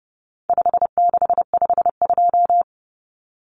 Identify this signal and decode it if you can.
Morse code, heard as a keyed tone, about 30 wpm, 700 Hz. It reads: 5652
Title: Common Cut Numbers - 4-Digit 30wpm